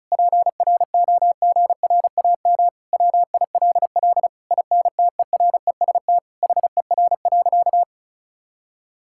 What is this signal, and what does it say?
Morse code, heard as a keyed tone, about 34 wpm, 700 Hz. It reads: PROGRAM WILL INTEREST HER.
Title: Sentences from Top 1000 Words - Review Part 5 - 35wpm